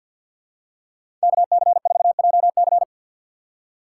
Morse code, heard as a keyed tone, about 50 wpm, 700 Hz. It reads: KC4JC
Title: USA Call Signs - 3x Character Spacing - 50wpm